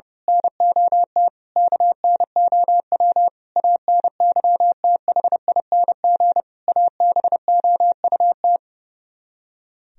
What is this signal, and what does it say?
Morse code, heard as a keyed tone, about 29 wpm, 700 Hz. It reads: NOT KNOW ANYTHING ABOUT
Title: Speed-Racing - Sets of 5 Words 30wpm